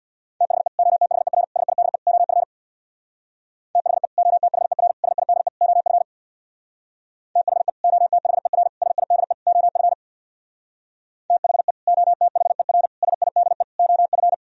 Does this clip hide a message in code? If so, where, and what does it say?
Yes: Morse code from 0.4 to 14.3 s, heard as a keyed tone, about 64 wpm, 700 Hz. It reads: THE OTHER SIDE OF THE OTHER SIDE OF THE OTHER SIDE OF THE OTHER SIDE OF